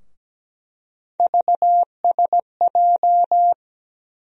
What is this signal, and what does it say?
Morse code, heard as a keyed tone, about 17 wpm, 700 Hz. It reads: VSJ
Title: USA Call Sign Postfixes - 3 Letters 17wpm